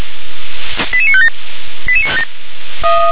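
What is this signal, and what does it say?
Five-tone selective call (ZVEI), heard as quick runs of five tones, tones of about 70 ms. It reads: 88957
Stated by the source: ZVEI five-tone selective call received by SDR.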